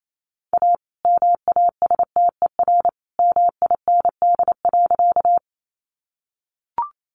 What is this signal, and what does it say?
Morse code, heard as a keyed tone, about 28 wpm, 700 Hz. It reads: A MASTER MIND.
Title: Sentences from Top 1300 Words 28wpm